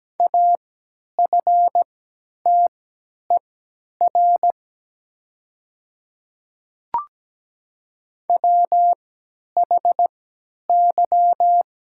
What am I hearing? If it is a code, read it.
Morse code, heard as a keyed tone, about 17 wpm, spaced as for 10 wpm, 700 Hz. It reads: AFTER WHY
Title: 3x Character Spacing - Top 200 Words - 17wpm